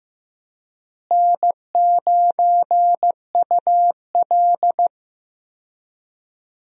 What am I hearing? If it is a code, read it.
Morse code, heard as a keyed tone, about 16 wpm, 700 Hz. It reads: N9UL
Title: USA Call Signs - 4 Characters - Repeated 3x - 15wpm